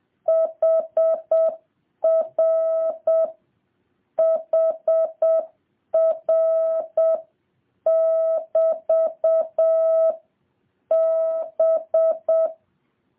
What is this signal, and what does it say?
Morse code, heard as a keyed tone, about 7 wpm, 650 Hz. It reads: HR HR <BT> B